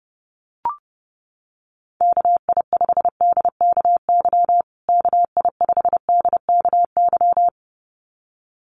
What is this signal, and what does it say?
Morse code, heard as a keyed tone, about 30 wpm, 700 Hz. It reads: KI5DKY KI5DKY
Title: USA Call Signs Repeated 2x 30wpm